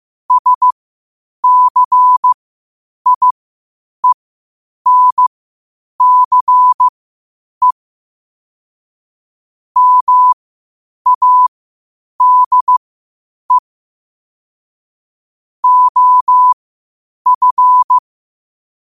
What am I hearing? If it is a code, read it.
Morse code, heard as a keyed tone, about 15 wpm, spaced as for 8 wpm, 1000 Hz. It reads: SCIENCE MADE OF